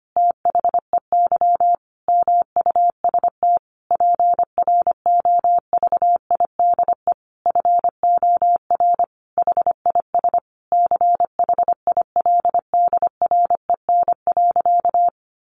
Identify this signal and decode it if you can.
Morse code, heard as a keyed tone, about 25 wpm, 700 Hz. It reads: THEY MUST PROVIDE FOR HIS CHILDREN.
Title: Sentences from Top 1100 Words 25wpm